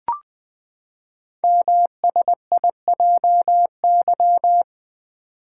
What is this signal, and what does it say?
Morse code, heard as a keyed tone, about 20 wpm, 700 Hz. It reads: MSIJY
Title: Code Groups - 5 Letters - 20wpm